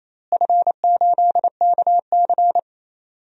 Morse code, heard as a keyed tone, about 28 wpm, 700 Hz. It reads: F8KC
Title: DXCC-100 Call Signs - 4 Characters - Repeated 3x - 28wpm